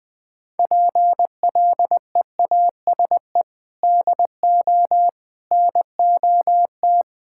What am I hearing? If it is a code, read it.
Morse code, heard as a keyed tone, about 20 wpm, 700 Hz. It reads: PLEASE DO NOT
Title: Sentences with 4-Word Phrases - 20wpm